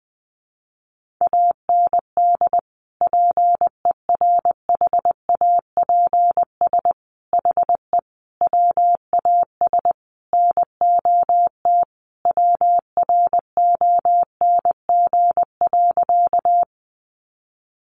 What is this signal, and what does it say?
Morse code, heard as a keyed tone, about 20 wpm, 700 Hz. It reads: AND PERHAPS HE WAS NOT WRONG.
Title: Sentences from Top 900 Words 20wpm